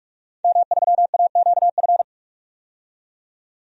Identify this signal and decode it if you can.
Morse code, heard as a keyed tone, about 44 wpm, 700 Hz. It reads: M3AQF
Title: DXCC-100 Call Signs - 5 Characters - 45wpm